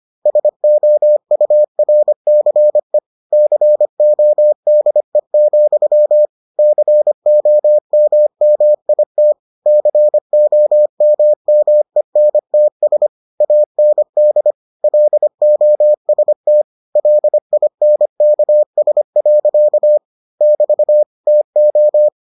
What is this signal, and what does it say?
Morse code, heard as a keyed tone, about 25 wpm, 600 Hz. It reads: SOURCE CODE, COMMIT COMMENTS AND LOST LINKS. <BT> TO